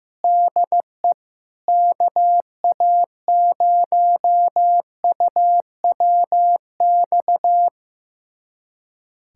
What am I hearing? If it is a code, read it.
Morse code, heard as a keyed tone, about 15 wpm, 700 Hz. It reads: DE KA0UWX
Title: USA Callsign DE Callsign - 6 Characters - 15wpm